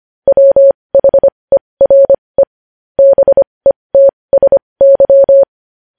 Morse code, heard as a keyed tone, about 25 wpm, spaced as for 20 wpm, 550 Hz. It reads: WHERE BETSY
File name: podcast / News headlines-1x-25x20